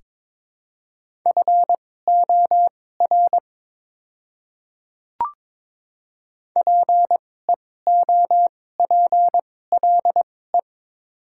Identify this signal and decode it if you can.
Morse code, heard as a keyed tone, about 22 wpm, spaced as for 16 wpm, 700 Hz. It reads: FOR PEOPLE
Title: 2x Character Spacing - Top 100 Words - 22wpm